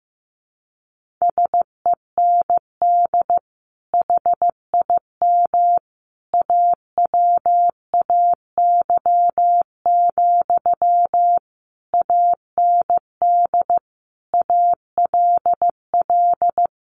Morse code, heard as a keyed tone, about 15 wpm, 700 Hz. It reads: SEND HIM AWAY, AND ALL
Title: Sentences from Top 1500 Words 15wpm